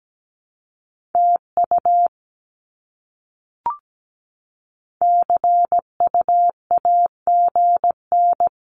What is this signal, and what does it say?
Morse code, heard as a keyed tone, about 17 wpm, 700 Hz. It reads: TU CUAGN
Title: Abbreviations Repeated 3x - 17wpm